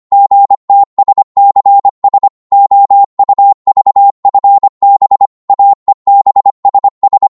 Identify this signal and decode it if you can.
Morse code, heard as a keyed tone, about 25 wpm, 800 Hz. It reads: GTSCS OUVFB AEBSS
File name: Morse / ltr-25wpm